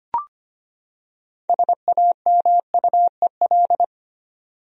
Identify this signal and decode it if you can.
Morse code, heard as a keyed tone, about 25 wpm, 700 Hz. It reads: SAMUEL
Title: USA Common Names - 25wpm